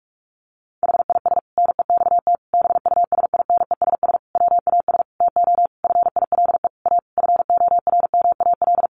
Morse code, heard as a keyed tone, about 44 wpm, 700 Hz. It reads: HIS NEXT BUSINESS WAS TO FILE A FORMAL
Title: Sentences from Top 1300 Words 45wpm